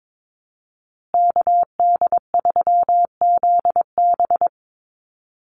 Morse code, heard as a keyed tone, about 22 wpm, 700 Hz. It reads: KD3ZB
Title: USA Call Signs Repeated 2x 22wpm